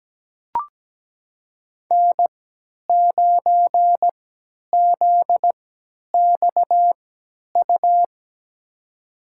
Morse code, heard as a keyed tone, about 17 wpm, spaced as for 10 wpm, 700 Hz. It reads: N9ZXU
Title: USA Call Signs - 5 Characters - 3x Character Spacing - 17wpm